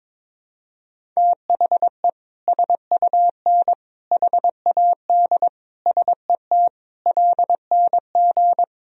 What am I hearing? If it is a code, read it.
Morse code, heard as a keyed tone, about 21 wpm, 700 Hz. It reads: THE SUN HAD SET LNG
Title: Sentences from Top 300 Words and Abbreviations 22wpm